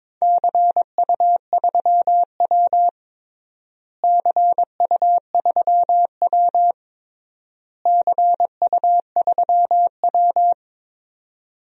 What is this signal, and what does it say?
Morse code, heard as a keyed tone, about 22 wpm, 700 Hz. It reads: CU3W CU3W CU3W
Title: DXCC-100 Call Signs - 4 Characters - Repeated 3x - 22wpm